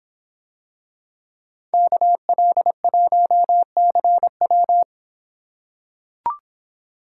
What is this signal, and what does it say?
Morse code, heard as a keyed tone, about 26 wpm, 700 Hz. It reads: KL1CW